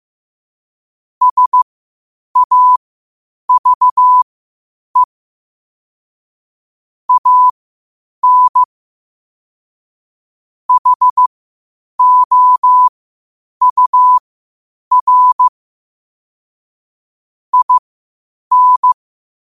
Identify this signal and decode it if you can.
Morse code, heard as a keyed tone, about 15 wpm, spaced as for 8 wpm, 1000 Hz. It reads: SAVE AN HOUR IN
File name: qotdfiles / 8wpm